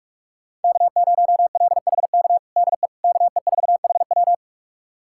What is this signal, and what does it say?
Morse code, heard as a keyed tone, about 42 wpm, 700 Hz. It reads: K0LHK DE KE4HW
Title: USA Callsign DE Callsign - 5 Characters - 45wpm